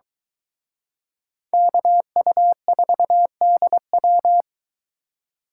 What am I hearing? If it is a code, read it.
Morse code, heard as a keyed tone, about 22 wpm, 700 Hz. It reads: KU4DW